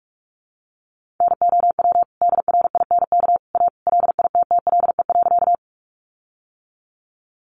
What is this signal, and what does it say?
Morse code, heard as a keyed tone, about 44 wpm, 700 Hz. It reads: NOW DRINK A LITTLE.
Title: Sentences from Top 1500 Words 45wpm